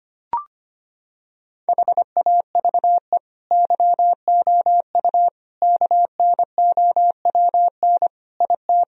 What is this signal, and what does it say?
Morse code, heard as a keyed tone, about 25 wpm, 700 Hz. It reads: HAVE YOU KNOWN IT
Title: Sentences from Top 1000 Words - Review Part 3 - 25wpm